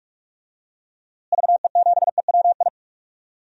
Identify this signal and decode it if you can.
Morse code, heard as a keyed tone, about 41 wpm, 700 Hz. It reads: VE7EWI
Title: DXCC-100 Call Signs - 6 Characters - Repeated 2x - 45wpm